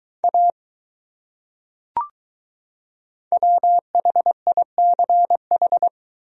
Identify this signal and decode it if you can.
Morse code, heard as a keyed tone, about 22 wpm, 700 Hz. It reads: A WHICH